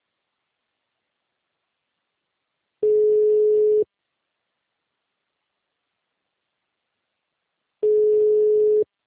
Ringback tone (Europe, 425 Hz).